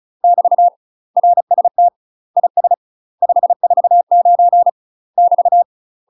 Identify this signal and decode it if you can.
Morse code, heard as a keyed tone, about 36 wpm, 700 Hz. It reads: <BT> RST IS 549 <BT>
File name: UK MASTER QSOs TRAINING / 35WPM / UK MASTER QSOs 0031